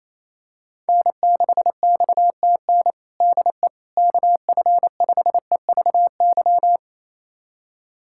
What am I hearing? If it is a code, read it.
Morse code, heard as a keyed tone, about 27 wpm, 700 Hz. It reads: N6XTN DE KF5EVY
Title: USA Callsign DE Callsign 28wpm